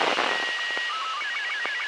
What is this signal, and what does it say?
SSTV (slow-scan television), Robot 36.
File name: arissat-aug05-2011